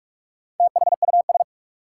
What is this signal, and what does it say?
Morse code, heard as a keyed tone, about 46 wpm, 700 Hz. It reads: THUS
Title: Sentences from Top 800 Words - New Word Drill - 45wpm